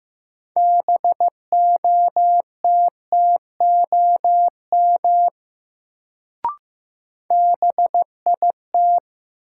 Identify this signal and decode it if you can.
Morse code, heard as a keyed tone, about 15 wpm, 700 Hz. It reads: BOTTOM BIT
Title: Sentences from Top 900 Words - New Word Drill - 15wpm